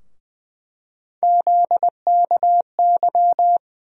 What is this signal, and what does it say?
Morse code, heard as a keyed tone, about 20 wpm, 700 Hz. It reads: ZKY
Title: USA Call Sign Postfixes - 3 Letters 20wpm